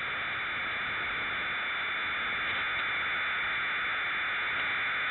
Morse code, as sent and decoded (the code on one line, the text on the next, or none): none